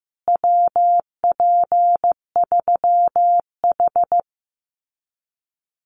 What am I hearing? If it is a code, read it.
Morse code, heard as a keyed tone, about 15 wpm, 700 Hz. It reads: WP3H
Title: Common QSO Phrases Encore 15wpm